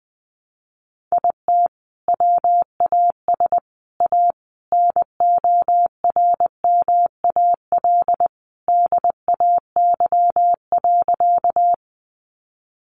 Morse code, heard as a keyed tone, about 20 wpm, 700 Hz. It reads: IT WAS A NORMAL DAY.
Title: Sentences from Top 1200 Words 20wpm